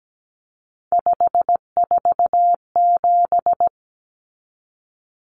Morse code, heard as a keyed tone, about 17 wpm, 700 Hz. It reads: 547
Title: Numbers - 3-Digit 17wpm